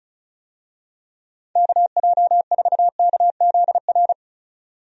Morse code, heard as a keyed tone, about 36 wpm, 700 Hz. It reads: KJ4KZR
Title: USA Call Signs - 6 Characters - 35wpm